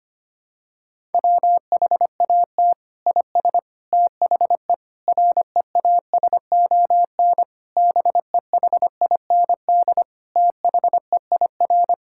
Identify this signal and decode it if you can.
Morse code, heard as a keyed tone, about 25 wpm, 700 Hz. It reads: WHAT IS THE REASON BEHIND THEIR
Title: Sentences from Top 1000 Words - Review Part 5 - 25wpm